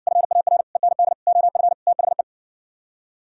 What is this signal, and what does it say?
Morse code, heard as a keyed tone, about 70 wpm, 700 Hz. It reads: FAR END OF THE